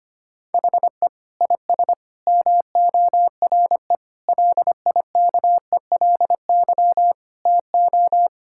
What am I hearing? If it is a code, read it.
Morse code, heard as a keyed tone, about 25 wpm, 700 Hz. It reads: HE IS MORE LIKELY TO
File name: practice / 25wpm / Sentences with 3-Word Phrases - 25wpm